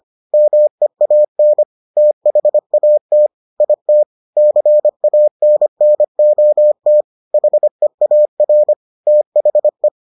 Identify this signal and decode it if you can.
Morse code, heard as a keyed tone, about 25 wpm, 600 Hz. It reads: MEAN THAT IT CANNOT HEAR THE